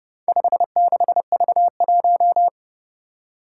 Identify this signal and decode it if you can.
Morse code, heard as a keyed tone, about 30 wpm, 700 Hz. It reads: 56V1
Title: Four Challenging Letter-Number Combinations 30wpm